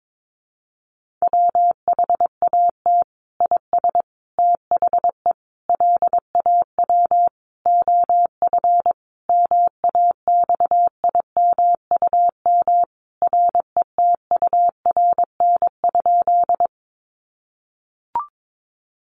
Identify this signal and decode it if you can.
Morse code, heard as a keyed tone, about 22 wpm, 700 Hz. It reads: WHAT IS THE LAW OF MAXIMUM RETURN?
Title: Sentences from Top 1300 Words 22wpm